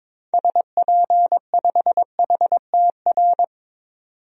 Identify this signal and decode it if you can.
Morse code, heard as a keyed tone, about 22 wpm, 700 Hz. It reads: SP5HTR
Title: DXCC-100 Call Signs - 6 Characters - 22wpm